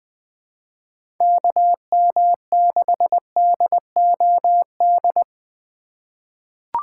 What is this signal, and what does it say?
Morse code, heard as a keyed tone, about 20 wpm, 700 Hz. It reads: KM6DOD
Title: USA Call Signs - 6 Characters - 3x Character Spacing - 20wpm